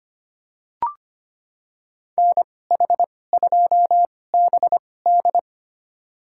Morse code, heard as a keyed tone, about 25 wpm, spaced as for 18 wpm, 700 Hz. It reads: NH2BD
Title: USA Call Signs - 5 Characters - 2x Character Spacing - 25wpm